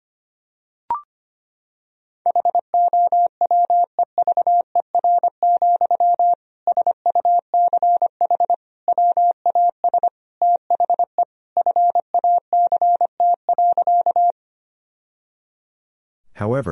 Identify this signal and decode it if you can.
Morse code, heard as a keyed tone, about 25 wpm, 700 Hz. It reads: HOWEVER, SUCH WAS THE FACT.
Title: Sentences from Top 1000 Words - Review Part 3 - 25wpm